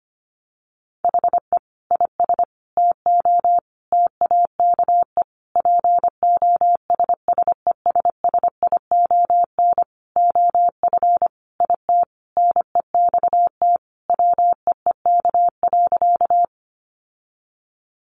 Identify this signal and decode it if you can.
Morse code, heard as a keyed tone, about 25 wpm, 700 Hz. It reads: HE IS TO TAKE POSSESSION OF IT NEXT WEEK.
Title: Sentences from Top 1400 Words 25wpm